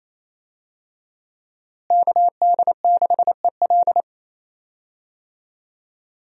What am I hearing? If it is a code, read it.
Morse code, heard as a keyed tone, about 28 wpm, 700 Hz. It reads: KD6EL
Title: USA Call Signs - 5 Characters - Repeated 2x - 28wpm